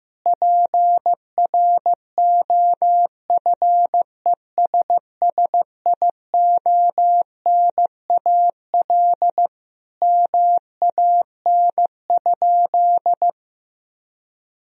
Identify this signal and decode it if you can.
Morse code, heard as a keyed tone, about 16 wpm, 700 Hz. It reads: PROFESSIONAL MAN?
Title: Sentences from Top 1000 Words - Review Part 4 - 15wpm